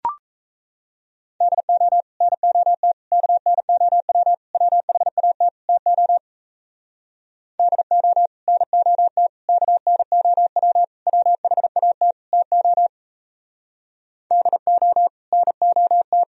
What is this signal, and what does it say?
Morse code, heard as a keyed tone, about 35 wpm, 700 Hz. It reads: DO NOT KNOW WHAT TO DO NOT KNOW WHAT TO DO NOT
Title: Speed-Racing - Sets of 5 Words 28wpm